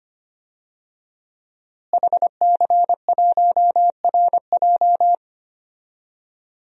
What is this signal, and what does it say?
Morse code, heard as a keyed tone, about 25 wpm, 700 Hz. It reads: HC1RJ
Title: DXCC-100 Call Signs - 5 Characters - Repeated 3x - 25wpm